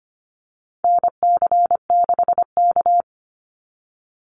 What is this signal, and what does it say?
Morse code, heard as a keyed tone, about 25 wpm, 700 Hz. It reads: NC6K
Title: Call Signs - 25wpm